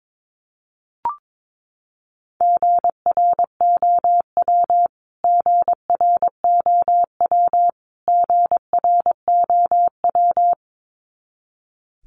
Morse code, heard as a keyed tone, about 22 wpm, 700 Hz. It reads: GROW GROW GROW
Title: Top 500 Words Repeated 22wpm